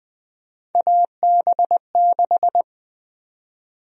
Morse code, heard as a keyed tone, about 20 wpm, 700 Hz. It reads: AB6
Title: USA Call Sign Prefixes - Repeated 2x - 20wpm